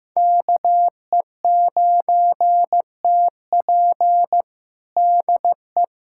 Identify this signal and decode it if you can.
Morse code, heard as a keyed tone, about 15 wpm, 700 Hz. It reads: KE9TP DE
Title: USA Callsign DE Callsign - 5 Characters - 15wpm